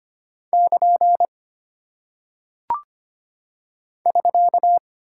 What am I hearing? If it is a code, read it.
Morse code, heard as a keyed tone, about 25 wpm, 700 Hz. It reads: <KN> <SK>
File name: Single Punctuation-Prosign - 25wpm